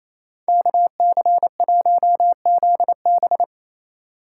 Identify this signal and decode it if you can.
Morse code, heard as a keyed tone, about 28 wpm, 700 Hz. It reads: KC1ZB
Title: USA Call Signs - 4x Character Spacing - 28wpm